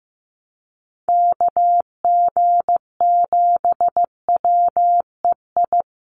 Morse code, heard as a keyed tone, about 15 wpm, 700 Hz. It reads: KG7WEI
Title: USA Call Signs Repeated 2x 15wpm